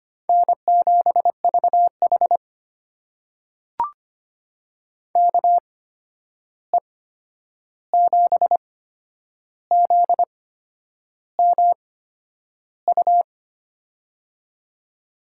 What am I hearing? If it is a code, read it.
Morse code, heard as a keyed tone, about 25 wpm, spaced as for 9 wpm, 700 Hz. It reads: N7VH KE7ZMU
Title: USA Call Signs - 8x Character Spacing - 25wpm